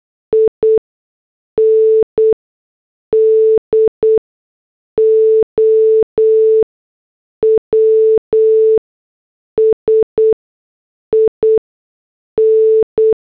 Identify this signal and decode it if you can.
Morse code, heard as a keyed tone, about 8 wpm, spaced as for 6 wpm, 440 Hz. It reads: INDOWS IN